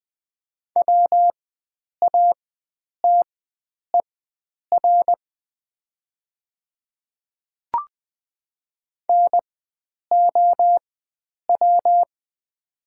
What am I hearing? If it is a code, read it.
Morse code, heard as a keyed tone, about 20 wpm, spaced as for 9 wpm, 700 Hz. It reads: WATER NOW
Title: 4x Character Spacing - Top 100 Words - 20wpm